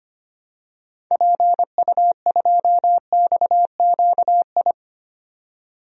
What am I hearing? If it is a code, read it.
Morse code, heard as a keyed tone, about 25 wpm, 700 Hz. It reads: PU2XQI
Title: DXCC-100 Call Signs - 6 Characters - 25wpm